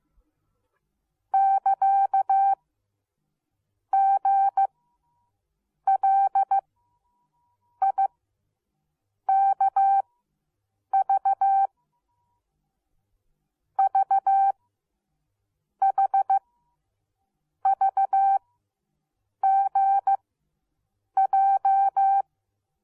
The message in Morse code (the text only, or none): <KA> GLIKV VHVGJ